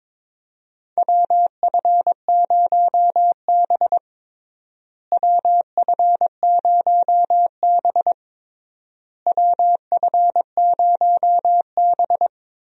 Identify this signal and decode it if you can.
Morse code, heard as a keyed tone, about 22 wpm, 700 Hz. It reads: WF0B WF0B WF0B
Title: USA Call Signs - 4 Characters - Repeated 3x - 22wpm